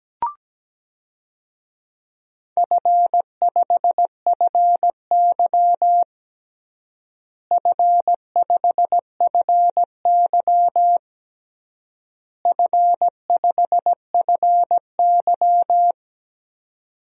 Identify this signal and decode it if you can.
Morse code, heard as a keyed tone, about 17 wpm, 700 Hz. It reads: F5FY F5FY F5FY
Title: DXCC-100 Call Signs - 4 Characters - Repeated 3x - 17wpm